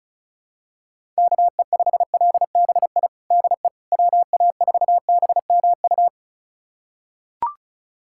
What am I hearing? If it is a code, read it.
Morse code, heard as a keyed tone, about 36 wpm, 700 Hz. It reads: KE5LBI DE WA4BMU
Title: USA Callsign DE Callsign - 6 Characters - 35wpm